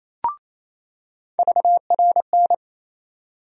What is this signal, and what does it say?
Morse code, heard as a keyed tone, about 28 wpm, 700 Hz. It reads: VRN
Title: USA Call Sign Postfixes - 3 Letters 28wpm